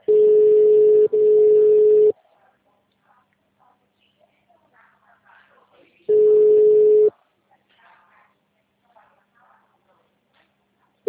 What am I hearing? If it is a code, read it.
Ringback tone (Europe, 425 Hz), with speech.